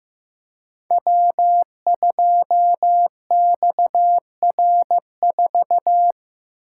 Morse code, heard as a keyed tone, about 15 wpm, 700 Hz. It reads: W2XR4
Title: Code Groups - 5 Letter:Numbers - 15wpm